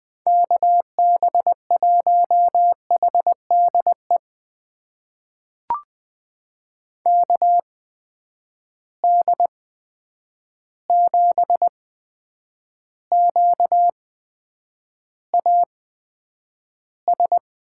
Morse code, heard as a keyed tone, about 20 wpm, spaced as for 9 wpm, 700 Hz. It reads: KB1HDE KD7QAS
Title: USA Call Signs - 6 Characters - 8x Character Spacing - 20wpm